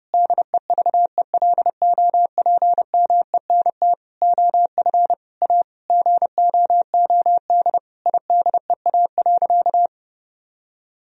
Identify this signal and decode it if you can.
Morse code, heard as a keyed tone, about 32 wpm, 700 Hz. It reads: DEVELOPMENT OF A GOOD IDEA.
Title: Sentences from Top 1000 Words - Review Part 4 - 30wpm